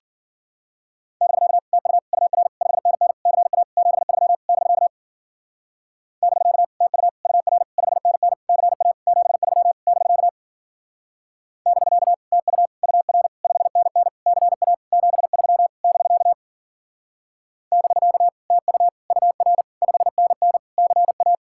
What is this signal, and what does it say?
Morse code, heard as a keyed tone, about 49 wpm, 700 Hz. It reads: <BK> TU UR 5NN CA 73 <BK> <BK> TU UR 5NN CA 73 <BK> <BK> TU UR 5NN CA 73 <BK> <BK> TU UR 5NN CA
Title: Speed-Racing Common QSO Phrases - 40wpm